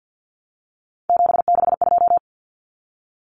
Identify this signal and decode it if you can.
Morse code, heard as a keyed tone, about 50 wpm, 700 Hz. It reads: 762
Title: Common Cut Numbers - 3-Digit 50wpm